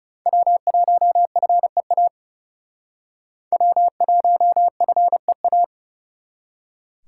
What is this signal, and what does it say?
Morse code, heard as a keyed tone, about 36 wpm, 700 Hz. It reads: W1FEA W1FEA
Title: Speed Racing - USA 1x3 Call Signs 30wpm